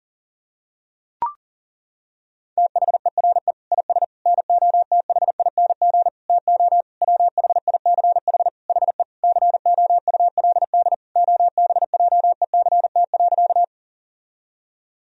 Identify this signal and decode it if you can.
Morse code, heard as a keyed tone, about 40 wpm, 700 Hz. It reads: THERE IS NOTHING TO WHICH HE COULD OBJECT.
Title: Sentences from Top 1000 Words - Review Part 3 - 40wpm